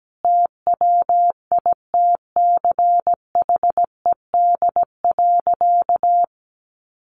Morse code, heard as a keyed tone, about 17 wpm, 700 Hz. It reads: TWITCHED.
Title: A Sound Of Thunder 17wpm